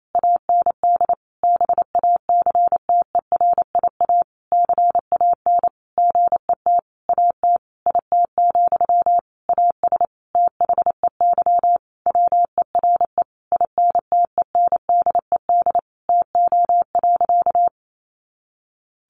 Morse code, heard as a keyed tone, about 28 wpm, 700 Hz. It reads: AND BACTERIA CAN GET AT IT, AS THEY WERE INTENDED TO.
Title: A Sound Of Thunder 28wpm